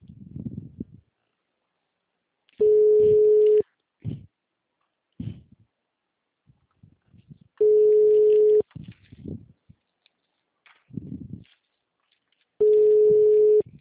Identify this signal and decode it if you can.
Ringback tone (Europe, 425 Hz).
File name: out- -2022- - - .71837